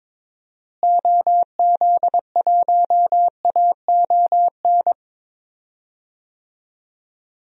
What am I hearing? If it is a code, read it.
Morse code, heard as a keyed tone, about 21 wpm, 700 Hz. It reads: OZ1AON
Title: DXCC-100 Call Signs - 6 Characters - Repeated 3x - 22wpm